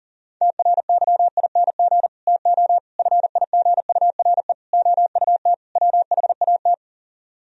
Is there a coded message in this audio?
Yes: Morse code from 0.4 to 6.7 s, heard as a keyed tone, about 52 wpm, 700 Hz. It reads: TRYING TO FIGURE OUT WHAT